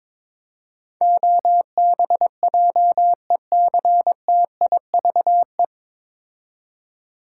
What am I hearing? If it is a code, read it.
Morse code, heard as a keyed tone, about 22 wpm, 700 Hz. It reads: OBJECTIVE
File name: 9 Letter Words - First 3 Letters Given - 22wpm